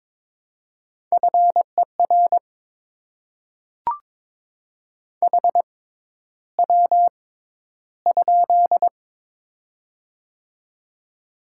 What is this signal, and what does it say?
Morse code, heard as a keyed tone, about 22 wpm, spaced as for 11 wpm, 700 Hz. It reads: FER HW?
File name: Top 20 QSO Elements - 6x Character Spacing - 22wpm